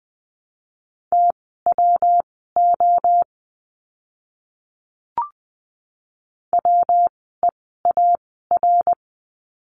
Morse code, heard as a keyed tone, about 20 wpm, spaced as for 14 wpm, 700 Hz. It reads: TWO WEAR
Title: 115 Lesson 19f - Letter w - Words 20wpm